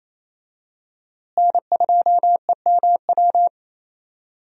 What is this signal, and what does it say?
Morse code, heard as a keyed tone, about 28 wpm, 700 Hz. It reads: N2EMW
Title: USA Call Signs - 5 Characters - Repeated 2x - 28wpm